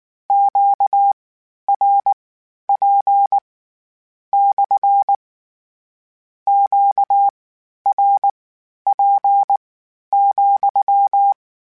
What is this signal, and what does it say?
Morse code, heard as a keyed tone, about 19 wpm, spaced as for 11 wpm, 800 Hz. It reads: QRP / QRP,
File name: Q Calls / Q Calls-mp3s / Q Calls SPSL2 @18